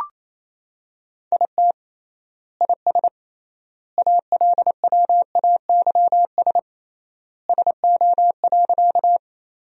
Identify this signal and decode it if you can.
Morse code, heard as a keyed tone, about 27 wpm, 700 Hz. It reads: IT IS ALWAYS SO.
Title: 3x Word-Spacing - Sentences from Top 300 Words - 28wpm